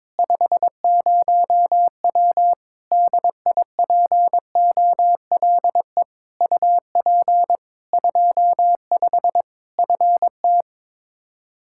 Morse code, heard as a keyed tone, about 22 wpm, 700 Hz. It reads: 50W DIPOLE UP 25 FT
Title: Practice QSO - Exchange 2 - Info - 22wpm